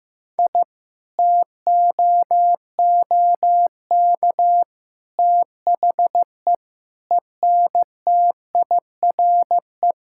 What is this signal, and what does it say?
Morse code, heard as a keyed tone, about 15 wpm, 700 Hz. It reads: I TOOK THE ENTIRE
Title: Sentences from Top 1000 Words - Review Part 5 - 15wpm